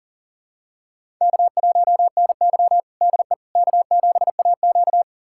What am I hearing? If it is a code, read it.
Morse code, heard as a keyed tone, about 40 wpm, 700 Hz. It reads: K1NY DE K7AQ
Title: USA Callsign DE Callsign - 4 Characters - 40wpm